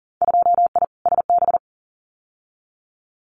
Morse code, heard as a keyed tone, about 41 wpm, 700 Hz. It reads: 2I SB
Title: Winter Field Day Exchanges Encore 40wpm